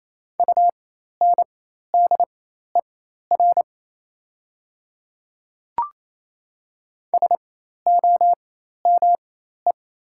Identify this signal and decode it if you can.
Morse code, heard as a keyed tone, about 27 wpm, spaced as for 13 wpm, 700 Hz. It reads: UNDER SOME
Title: 4x Character Spacing - Top 300 Words - 28wpm